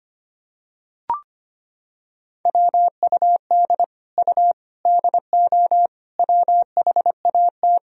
Morse code, heard as a keyed tone, about 25 wpm, 700 Hz. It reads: WUD U DO WHAT
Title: Sentences from Top 100 Words and Abbreviations 25wpm